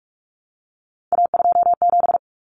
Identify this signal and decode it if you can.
Morse code, heard as a keyed tone, about 44 wpm, 700 Hz. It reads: A27
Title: Common Cut Numbers - 3-Digit 45wpm